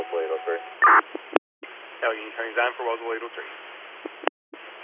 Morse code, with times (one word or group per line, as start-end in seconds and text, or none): none